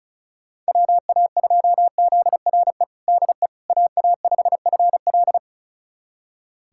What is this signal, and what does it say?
Morse code, heard as a keyed tone, about 36 wpm, 700 Hz. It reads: WA2ZRE DE AA5FL
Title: USA Callsign DE Callsign 35wpm